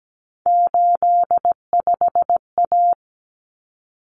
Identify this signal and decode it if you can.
Morse code, heard as a keyed tone, about 17 wpm, 700 Hz. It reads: 85A
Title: Common Cut Numbers - 3-Digit 17wpm